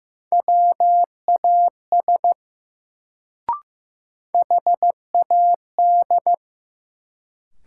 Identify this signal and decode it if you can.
Morse code, heard as a keyed tone, about 15 wpm, 700 Hz. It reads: WAS HAD
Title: Sentences from Top 100 Words - New Word Drill - 15wpm